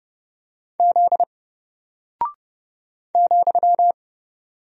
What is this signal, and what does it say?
Morse code, heard as a keyed tone, about 30 wpm, 700 Hz. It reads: Z ,
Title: Single Character (Letter:Number:Punctuation:Prosign) Alt 30wpm